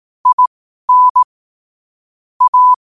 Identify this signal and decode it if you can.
Morse code, heard as a keyed tone, about 18 wpm, spaced as for 12 wpm, 1000 Hz. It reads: IN A
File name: qotdfiles / 12wpm